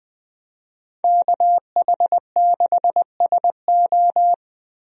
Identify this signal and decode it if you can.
Morse code, heard as a keyed tone, about 20 wpm, 700 Hz. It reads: KH6SO
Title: USA Call Signs - 5 Characters - Repeated 3x - 20wpm